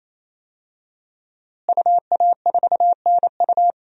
Morse code, heard as a keyed tone, about 28 wpm, 700 Hz. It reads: UA4NU
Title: DXCC-100 Call Signs - 5 Characters - Repeated 3x - 28wpm